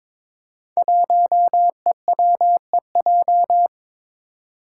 Morse code, heard as a keyed tone, about 22 wpm, 700 Hz. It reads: 1EWEJ
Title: Five Challenging Letter-Number Combinations 22wpm